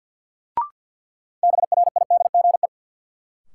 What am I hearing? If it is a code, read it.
Morse code, heard as a keyed tone, about 48 wpm, 700 Hz. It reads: BRIDGE
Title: Sentences from Top 1200 Words - New Word Drill Alt - 50wpm